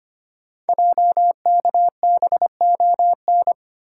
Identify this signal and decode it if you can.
Morse code, heard as a keyed tone, about 25 wpm, 690 Hz. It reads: JKBON
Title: Code Groups - 5 Letter:Numbers - 25wpm